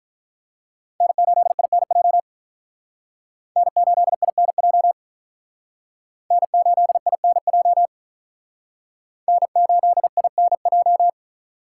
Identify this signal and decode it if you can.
Morse code, heard as a keyed tone, about 44 wpm, 700 Hz. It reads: N8INJ N8INJ N8INJ N8INJ